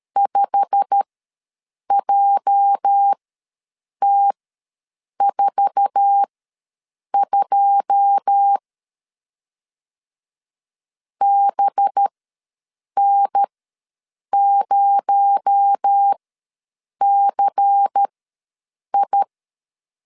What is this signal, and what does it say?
Morse code, heard as a keyed tone, about 13 wpm, spaced as for 7 wpm, 800 Hz. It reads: 5JT42 BN0CI
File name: LEZIONI / 9 2 ws 12wpm 3406